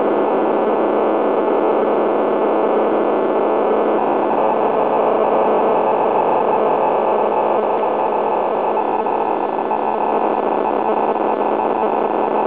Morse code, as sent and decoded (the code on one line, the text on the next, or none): none